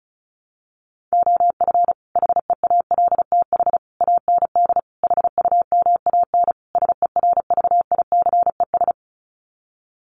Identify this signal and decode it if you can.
Morse code, heard as a keyed tone, about 34 wpm, 700 Hz. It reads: OF HEALTH AND HUMAN SERVICES
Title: Sets of 5 Words 35wpm